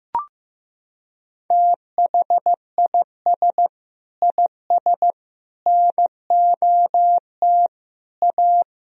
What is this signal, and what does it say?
Morse code, heard as a keyed tone, about 15 wpm, 700 Hz. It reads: THIS IS NOT A
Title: Sentences from Top 1000 Words - Review Part 2 - 15wpm